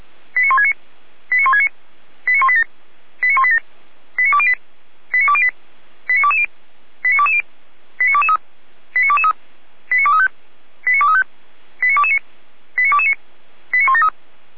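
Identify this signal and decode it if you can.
Five-tone selective call (ZVEI), heard as quick runs of five tones, tones of about 70 ms. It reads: 89169 89169 89187 89187 89208 89208 89220 89220 89223 89223 89235 89235 89229 89229 89173
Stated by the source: ZVEI five-tone call received by SDR.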